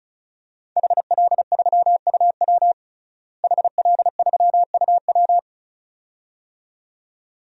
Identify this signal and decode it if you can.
Morse code, heard as a keyed tone, about 35 wpm, 700 Hz. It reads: HL3UW HL3UW
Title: DXCC-100 Call Signs - 5 Characters - Repeated 2x - 35wpm